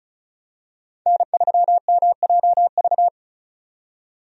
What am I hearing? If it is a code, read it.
Morse code, heard as a keyed tone, about 34 wpm, 700 Hz. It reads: N3MJV
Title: USA Call Signs - 5 Characters - 35wpm